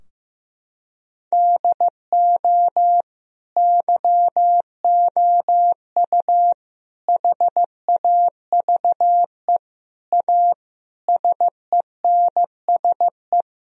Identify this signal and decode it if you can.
Morse code, heard as a keyed tone, about 16 wpm, 700 Hz. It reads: DO YOU HAVE A SENSE
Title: Sentences from Top 1000 Words - Review Part 4 - 15wpm